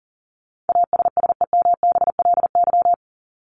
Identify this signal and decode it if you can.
Morse code, heard as a keyed tone, about 40 wpm, 700 Hz. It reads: ASSEMBLY
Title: LY Suffix - Top 100 Words with Base Words 40wpm